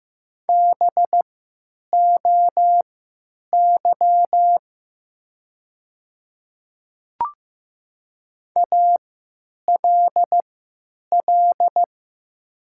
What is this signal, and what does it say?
Morse code, heard as a keyed tone, about 15 wpm, spaced as for 9 wpm, 700 Hz. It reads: BOY ALL
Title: 3x Character Spacing - Top 300 Words - 15wpm